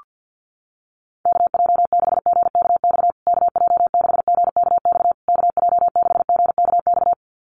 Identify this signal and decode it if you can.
Morse code, heard as a keyed tone, about 48 wpm, 700 Hz. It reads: KJ6GKX KJ6GKX KJ6GKX
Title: USA Call Signs Repeated 3x 50wpm